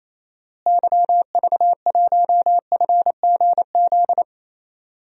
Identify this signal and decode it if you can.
Morse code, heard as a keyed tone, about 28 wpm, 700 Hz. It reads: YV1FGZ
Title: DXCC-100 Call Signs - 6 Characters - Repeated 3x - 28wpm